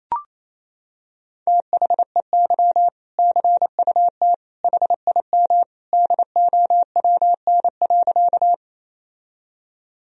Morse code, heard as a keyed tone, about 28 wpm, 700 Hz. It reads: THEY CUT HIM DOWN.
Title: Sentences from Top 1000 Words - Review Part 1 - 28wpm